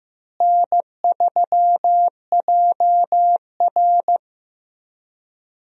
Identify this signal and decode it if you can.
Morse code, heard as a keyed tone, about 15 wpm, 700 Hz. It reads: N3JR
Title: USA Call Signs - 4 Characters - Repeated 3x - 15wpm